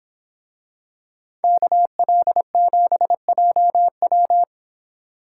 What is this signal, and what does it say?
Morse code, heard as a keyed tone, about 26 wpm, 700 Hz. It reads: KL7JW